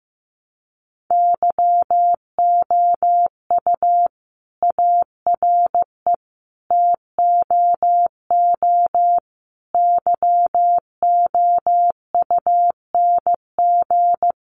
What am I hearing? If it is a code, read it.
Morse code, heard as a keyed tone, about 15 wpm, 700 Hz. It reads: YOU ARE TOO YOUNG
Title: Sentences from Top 1400 Words 15wpm